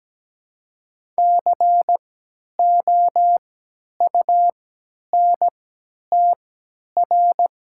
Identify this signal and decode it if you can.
Morse code, heard as a keyed tone, about 17 wpm, spaced as for 10 wpm, 700 Hz. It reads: COUNTR
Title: 3x Character Spacing - Top 300 Words - 17wpm